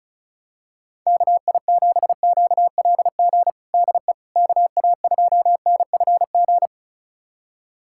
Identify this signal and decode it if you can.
Morse code, heard as a keyed tone, about 36 wpm, 690 Hz. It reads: KI7QLG DE KA2NFG
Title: USA Callsign DE Callsign - 6 Characters - 35wpm